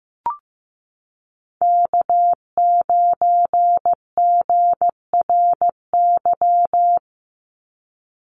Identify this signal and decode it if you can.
Morse code, heard as a keyed tone, about 15 wpm, 700 Hz. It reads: K9GRY
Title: USA Call Signs 15wpm